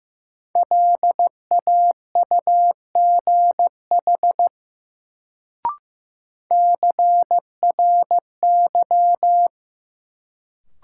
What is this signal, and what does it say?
Morse code, heard as a keyed tone, about 15 wpm, 700 Hz. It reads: LAUGH CRY
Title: Sentences from Top 500 Words - New Word Drill - 15wpm